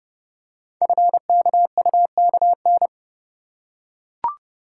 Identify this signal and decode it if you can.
Morse code, heard as a keyed tone, about 30 wpm, 700 Hz. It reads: FKUKN
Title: Code Groups - 5 Letters - 30wpm